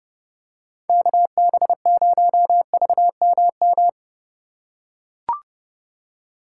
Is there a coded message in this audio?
Yes: Morse code from 0.9 to 3.9 s, heard as a keyed tone, about 27 wpm, 700 Hz. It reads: KB0VMM